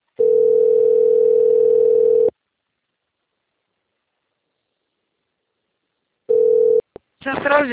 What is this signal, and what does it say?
Ringback tone (North America), 440 + 480 Hz.